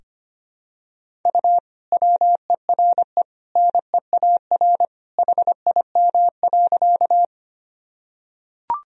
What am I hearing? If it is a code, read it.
Morse code, heard as a keyed tone, about 25 wpm, 700 Hz. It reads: U WERE NEAR HIM.
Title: Sentences from Top 200 Words and Abbreviations 25wpm